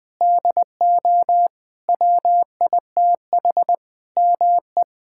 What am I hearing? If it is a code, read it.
Morse code, heard as a keyed tone, about 20 wpm, 700 Hz. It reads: DO WITH ME
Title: Sentences from Top 1600 Words 20wpm